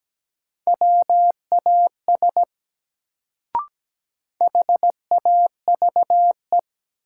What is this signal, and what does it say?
Morse code, heard as a keyed tone, about 17 wpm, 700 Hz. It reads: WAS HAVE
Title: Sentences from Top 100 Words - New Word Drill - 17wpm